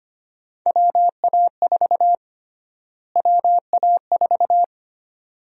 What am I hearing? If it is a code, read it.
Morse code, heard as a keyed tone, about 25 wpm, 700 Hz. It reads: WA4 WA4
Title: USA Call Sign Prefixes - Repeated 2x - 25wpm